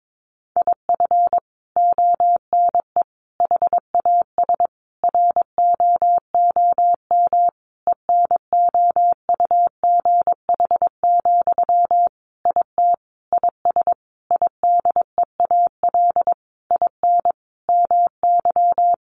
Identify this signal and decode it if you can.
Morse code, heard as a keyed tone, about 22 wpm, 700 Hz. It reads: IF ONE HAS ROOM ENOUGH, IT IS IDEAL IN MY
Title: Sentences from Top 1400 Words 22wpm